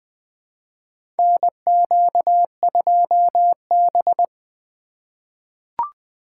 Morse code, heard as a keyed tone, about 20 wpm, 700 Hz. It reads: NQ2B
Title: USA Call Signs - 4 Characters - 2x Character Spacing - 20wpm